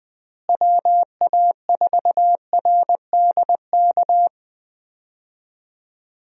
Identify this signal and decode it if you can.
Morse code, heard as a keyed tone, about 20 wpm, 700 Hz. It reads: WA4RDK
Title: USA Call Signs - 6 Characters - Repeated 3x - 20wpm